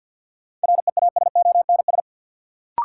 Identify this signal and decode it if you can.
Morse code, heard as a keyed tone, about 52 wpm, 700 Hz. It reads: REASONS
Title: Sets of 4 Words - New Word Drill - 50wpm